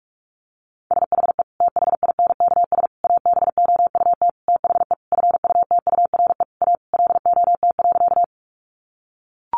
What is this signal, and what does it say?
Morse code, heard as a keyed tone, about 44 wpm, 700 Hz. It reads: SHE THINKS ABOUT THE FUTURE A LOT.
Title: Easy Phrases 45wpm